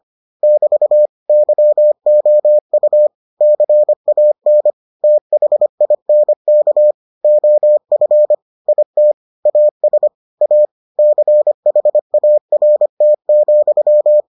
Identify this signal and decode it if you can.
Morse code, heard as a keyed tone, about 25 wpm, 600 Hz. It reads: <BT> YOU CAN THINK OF IT AS A CHART,